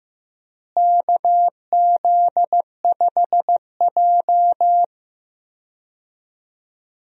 Morse code, heard as a keyed tone, about 15 wpm, 700 Hz. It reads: KZ5J
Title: USA Call Signs - 4 Characters - Repeated 2x - 15wpm